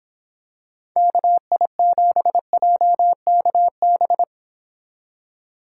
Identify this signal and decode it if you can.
Morse code, heard as a keyed tone, about 26 wpm, 700 Hz. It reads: KI7JKB